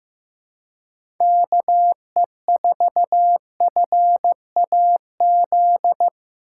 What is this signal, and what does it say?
Morse code, heard as a keyed tone, about 15 wpm, 700 Hz. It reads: KE4FAZ
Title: USA Call Signs - 6 Characters - 4x Character Spacing - 15wpm